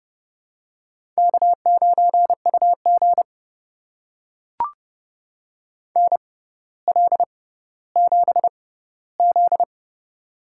Morse code, heard as a keyed tone, about 32 wpm, spaced as for 16 wpm, 700 Hz. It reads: K9UG NL7Z
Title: USA Call Signs - 4 Characters - 6x Character Spacing - 30wpm